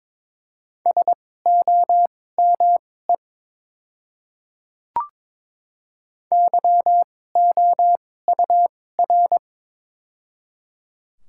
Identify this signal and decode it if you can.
Morse code, heard as a keyed tone, about 22 wpm, spaced as for 16 wpm, 700 Hz. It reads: SOME YOUR
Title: 2x Character Spacing - Top 100 Words - 22wpm